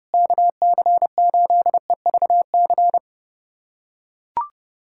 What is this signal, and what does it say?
Morse code, heard as a keyed tone, about 32 wpm, 700 Hz. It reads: KC8EVC
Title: USA Callsign DE Callsign 30wpm